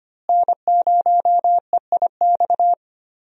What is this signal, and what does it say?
Morse code, heard as a keyed tone, about 25 wpm, 700 Hz. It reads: N0EIX
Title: USA Call Signs - 5 Characters - Repeated 3x - 25wpm